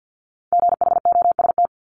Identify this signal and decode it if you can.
Morse code, heard as a keyed tone, about 50 wpm, 690 Hz. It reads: GHOST
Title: Multi-letter Phonemes Repeated 3x 50wpm